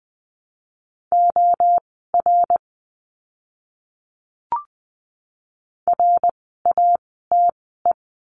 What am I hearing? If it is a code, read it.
Morse code, heard as a keyed tone, about 20 wpm, spaced as for 14 wpm, 700 Hz. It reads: OR RATE
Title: 101 Lesson 17f - Letter u - Words 20wpm